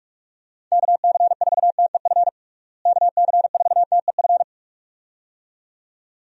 Morse code, heard as a keyed tone, about 46 wpm, 700 Hz. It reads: KC4TEF KC4TEF
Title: USA Call Signs - 6 Characters - Repeated 2x - 45wpm